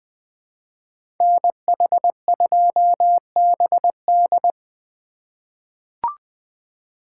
Morse code, heard as a keyed tone, about 20 wpm, 700 Hz. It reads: NH2BD
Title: USA Call Signs - 5 Characters - 2x Character Spacing - 20wpm